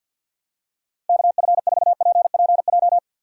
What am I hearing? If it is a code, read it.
Morse code, heard as a keyed tone, about 52 wpm, 700 Hz. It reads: KF4PPJ
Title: USA Call Signs - 6 Characters - 4x Character Spacing - 50wpm